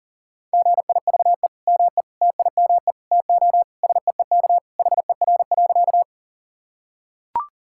Morse code, heard as a keyed tone, about 37 wpm, 700 Hz. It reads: GIVE ME TIME TO SEEK HER.
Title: Sentences from Top 1600 Words 40wpm